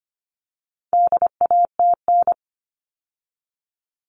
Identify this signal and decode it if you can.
Morse code, heard as a keyed tone, about 25 wpm, 700 Hz. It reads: DATN
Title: All Cut Numbers - 4-Digit 25wpm